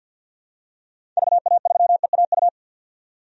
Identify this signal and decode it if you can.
Morse code, heard as a keyed tone, about 51 wpm, 700 Hz. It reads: VA3EAU
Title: DXCC-100 Call Signs - 6 Characters - 50wpm